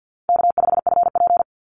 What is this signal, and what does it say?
Morse code, heard as a keyed tone, about 50 wpm, 700 Hz. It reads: K5FP
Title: Easy Call Signs - 50wpm